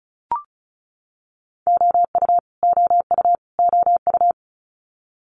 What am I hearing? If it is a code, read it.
Morse code, heard as a keyed tone, about 36 wpm, 700 Hz. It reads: OU OU OU
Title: Multi-letter Phonemes Repeated 3x 35wpm